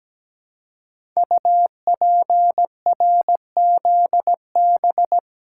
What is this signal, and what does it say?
Morse code, heard as a keyed tone, about 17 wpm, 700 Hz. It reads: UPRZB
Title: Code Groups - 5 Letters - 17wpm